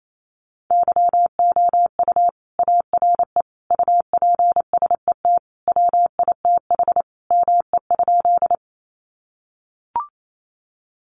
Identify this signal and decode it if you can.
Morse code, heard as a keyed tone, about 28 wpm, 700 Hz. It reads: YOU ARE UPSET WITH ME?
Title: Sentences from Top 800 Words 28wpm